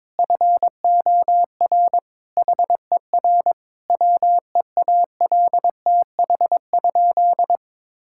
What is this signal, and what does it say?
Morse code, heard as a keyed tone, about 22 wpm, 700 Hz. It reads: FOR HER WEALTH?
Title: Sentences from Top 1600 Words 22wpm